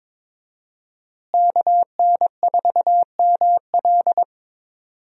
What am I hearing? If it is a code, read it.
Morse code, heard as a keyed tone, about 21 wpm, 700 Hz. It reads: KN4ML
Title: USA Call Signs - 5 Characters - Repeated 2x - 22wpm